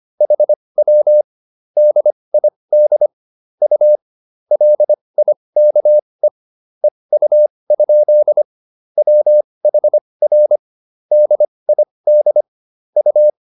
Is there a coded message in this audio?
Yes: Morse code from 0.2 to 13.3 s, heard as a keyed tone, about 25 wpm, spaced as for 20 wpm, 600 Hz. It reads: HW DID U LIKE EU? WHR DID U